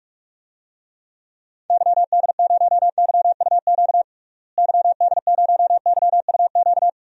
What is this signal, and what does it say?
Morse code, heard as a keyed tone, about 46 wpm, 700 Hz. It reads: YD0YUQ YD0YUQ
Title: DXCC-100 Call Signs - 6 Characters - Repeated 2x - 45wpm